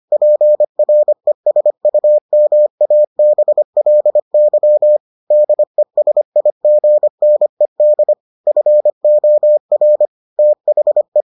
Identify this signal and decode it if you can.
Morse code, heard as a keyed tone, about 25 wpm, 600 Hz. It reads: PRESUMABLY DESIGNED FOR THE